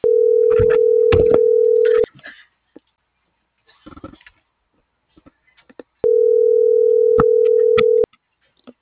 Ringback tone (North America), 440 + 480 Hz.